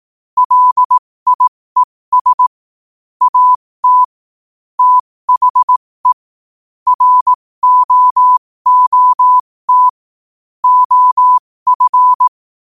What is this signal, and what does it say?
Morse code, heard as a keyed tone, about 18 wpm, spaced as for 15 wpm, 1000 Hz. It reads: LIES AT THE ROOT OF
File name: qotdfiles / 15wpm